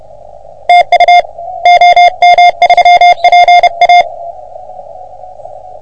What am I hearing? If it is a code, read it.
Morse code, heard as a keyed tone, about 31 wpm, 710 Hz. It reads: TU OM3PA
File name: OM3PA PILUP CUC 388